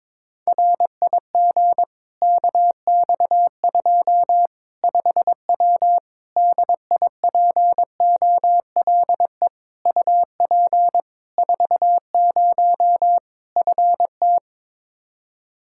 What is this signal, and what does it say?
Morse code, heard as a keyed tone, about 22 wpm, 700 Hz. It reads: RIG KX2 5W DIPOLE UP 40 FT
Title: Practice QSO - Exchange 2 - Info - 22wpm